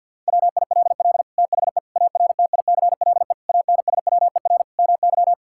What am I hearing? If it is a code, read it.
Morse code, heard as a keyed tone, about 50 wpm, 700 Hz. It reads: WILL THE ARTICLE ANSWER MY
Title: Sentences from Top 1000 Words - Review Part 5 - 50wpm